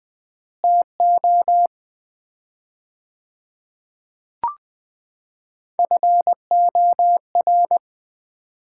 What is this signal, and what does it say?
Morse code, heard as a keyed tone, about 20 wpm, 700 Hz. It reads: TO FOR
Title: 4x Word-Spacing - Sets of 3 Words - 20wpm